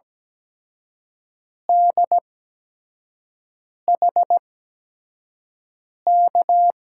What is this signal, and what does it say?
Morse code, heard as a keyed tone, about 17 wpm, spaced as for 5 wpm, 700 Hz. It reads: DHK
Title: USA Call Signs - 5 Characters - 8x Character Spacing - 17wpm